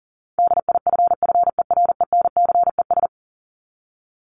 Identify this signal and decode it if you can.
Morse code, heard as a keyed tone, about 40 wpm, 700 Hz. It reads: DIFFERENCES
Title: ES:S Suffix - Top 100 Words with Base Words 40wpm